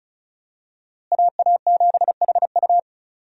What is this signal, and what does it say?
Morse code, heard as a keyed tone, about 34 wpm, 700 Hz. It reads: AA7HU
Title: USA Call Signs - 5 Characters - Repeated 2x - 35wpm